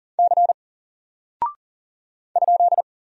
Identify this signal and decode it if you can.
Morse code, heard as a keyed tone, about 39 wpm, 700 Hz. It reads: C ?
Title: Single Character (Letter:Number:Punctuation:Prosign) 40wpm